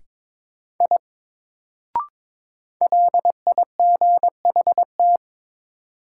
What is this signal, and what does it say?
Morse code, heard as a keyed tone, about 22 wpm, 700 Hz. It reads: I LIGHT
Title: Sentences from Top 200 Words - New Word Drill - 22wpm